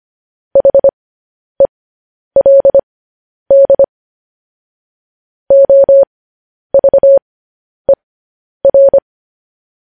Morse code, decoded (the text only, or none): HELD OVER